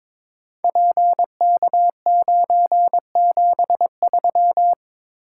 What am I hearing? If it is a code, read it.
Morse code, heard as a keyed tone, about 22 wpm, 700 Hz. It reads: PK973
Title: Code Groups - 5 Letter:Numbers - 22wpm